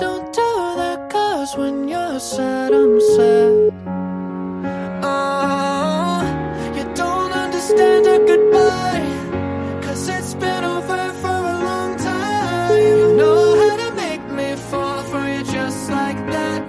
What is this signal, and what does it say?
Ringback tone (Europe, 425 Hz), with noise.